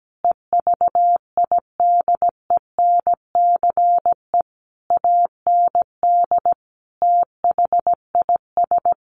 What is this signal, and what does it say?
Morse code, heard as a keyed tone, about 17 wpm, 700 Hz. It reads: EVIDENCE AND THIS
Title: A Sound Of Thunder 17wpm